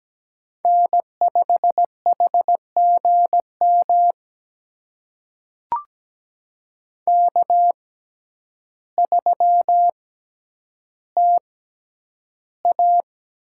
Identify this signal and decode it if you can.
Morse code, heard as a keyed tone, about 17 wpm, spaced as for 9 wpm, 700 Hz. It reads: N5HGM K3TA
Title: USA Call Signs - 6x Character Spacing - 17wpm